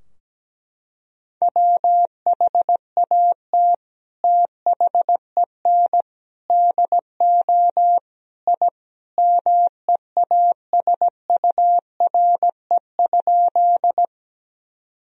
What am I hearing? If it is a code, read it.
Morse code, heard as a keyed tone, about 17 wpm, 700 Hz. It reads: WHAT THEN DO I MEASURE?
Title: Sentences from Top 1000 Words - Review Part 2 - 17wpm